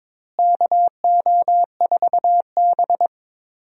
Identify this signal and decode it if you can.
Morse code, heard as a keyed tone, about 22 wpm, 700 Hz. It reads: KO4B
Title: USA Call Signs - 4 Characters - Repeated 3x - 22wpm